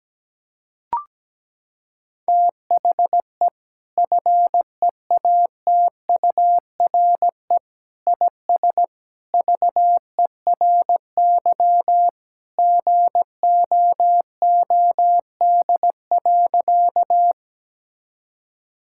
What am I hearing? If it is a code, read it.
Morse code, heard as a keyed tone, about 17 wpm, 700 Hz. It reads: THE FEATURE IS VERY GOOD.
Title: Sentences from Top 1000 Words - Review Part 5 - 17wpm